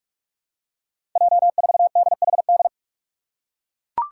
Morse code, heard as a keyed tone, about 42 wpm, 700 Hz. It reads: J4DHD
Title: Five Challenging Letter-Number Combinations 45wpm